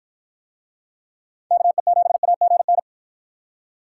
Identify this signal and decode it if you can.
Morse code, heard as a keyed tone, about 52 wpm, 700 Hz. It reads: KE7AGN